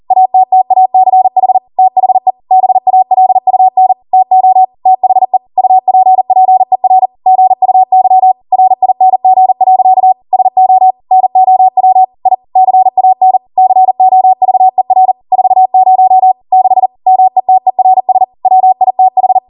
Morse code, heard as a keyed tone, about 40 wpm, 750 Hz. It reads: ATTACH THE BALUN TO THE UPPER GUY RING. SO NOW I CAN COVER 40 6 METERS WITH
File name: Morse / 40WPM